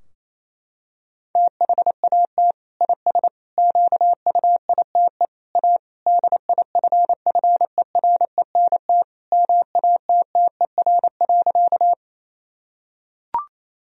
Morse code, heard as a keyed tone, about 27 wpm, 700 Hz. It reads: THAT IS QUITE A DIFFERENT MATTER.
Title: Sentences from Top 1000 Words - Review Part 3 - 28wpm